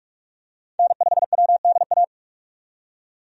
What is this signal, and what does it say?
Morse code, heard as a keyed tone, about 46 wpm, 700 Hz. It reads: N5WDA
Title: USA Call Signs - 5 Characters - 45wpm